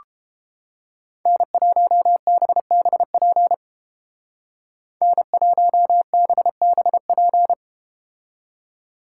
Morse code, heard as a keyed tone, about 30 wpm, 700 Hz. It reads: N1BBP N1BBP